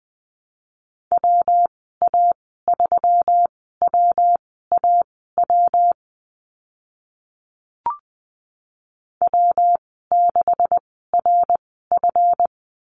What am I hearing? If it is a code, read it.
Morse code, heard as a keyed tone, about 20 wpm, spaced as for 14 wpm, 700 Hz. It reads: WA3WAW W6RF
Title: 140 Lesson 23g - Letter f - Call Signs 20wpm